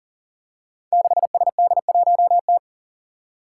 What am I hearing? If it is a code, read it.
Morse code, heard as a keyed tone, about 40 wpm, 700 Hz. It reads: 6SD1T
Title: Five Challenging Letter-Number Combinations 40wpm